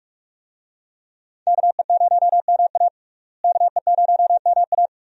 Morse code, heard as a keyed tone, about 46 wpm, 700 Hz. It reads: KE0MA KE0MA
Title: USA Call Signs - 5 Characters - Repeated 3x - 45wpm